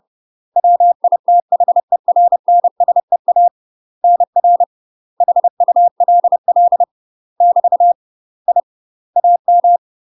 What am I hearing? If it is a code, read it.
Morse code, heard as a keyed tone, about 30 wpm, 700 Hz. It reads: WITHERNSEA NR HULL <BT> I AM
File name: UK MASTER QSOs TRAINING / 30WPM / UK MASTER QSOs 0011